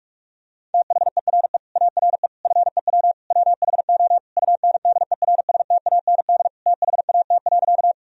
Morse code, heard as a keyed tone, about 41 wpm, 700 Hz. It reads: THERE ARE FEW WHO UNDERSTAND THAT.
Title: Sentences from Top 1000 Words - Review Part 1 - 45wpm